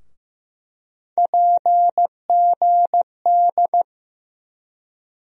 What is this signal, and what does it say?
Morse code, heard as a keyed tone, about 15 wpm, 700 Hz. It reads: PGD
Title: USA Call Sign Postfixes - 3 Letters 15wpm